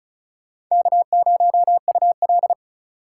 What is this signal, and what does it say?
Morse code, heard as a keyed tone, about 46 wpm, 700 Hz. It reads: K0UL